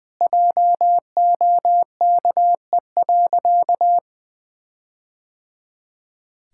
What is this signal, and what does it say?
Morse code, heard as a keyed tone, about 21 wpm, 700 Hz. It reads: JOKE.
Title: All-Age Appropriate Jokes - 20wpm